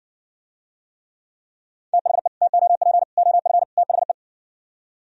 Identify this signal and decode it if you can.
Morse code, heard as a keyed tone, about 57 wpm, 700 Hz. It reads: THE TOP OF THE